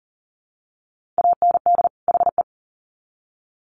Morse code, heard as a keyed tone, about 40 wpm, 700 Hz. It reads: AND HE
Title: Sets of 2 Words 40wpm